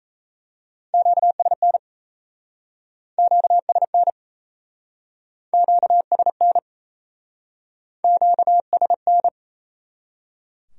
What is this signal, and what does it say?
Morse code, heard as a keyed tone, about 38 wpm, 700 Hz. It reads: QSN QSN QSN QSN